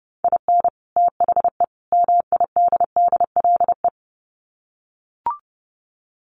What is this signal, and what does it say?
Morse code, heard as a keyed tone, about 30 wpm, 700 Hz. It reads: IN THE MIDDLE
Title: Sets of 3 Words 30wpm